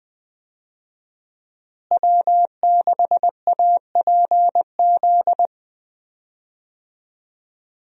Morse code, heard as a keyed tone, about 20 wpm, 700 Hz. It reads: W6APZ
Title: USA Call Signs - 5 Characters - Repeated 3x - 20wpm